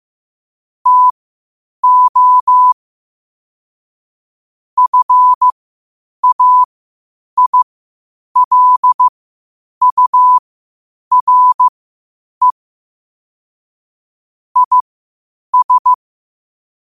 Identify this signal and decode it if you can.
Morse code, heard as a keyed tone, about 15 wpm, spaced as for 8 wpm, 1000 Hz. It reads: TO FAILURE IS
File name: qotdfiles / 8wpm